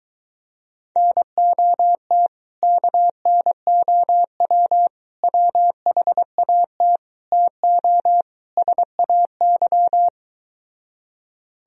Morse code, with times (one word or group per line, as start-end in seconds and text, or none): 1.0-2.3 NOT
2.6-4.9 KNOW
5.2-7.0 WHAT
7.3-8.2 TO
8.6-10.1 SAY